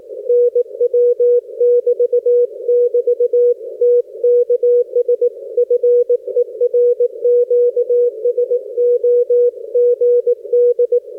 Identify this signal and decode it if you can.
Morse code, heard as a keyed tone, about 18 wpm, 480 Hz. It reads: NW <BT><BT> TKS FER QSO GD
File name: 60m qso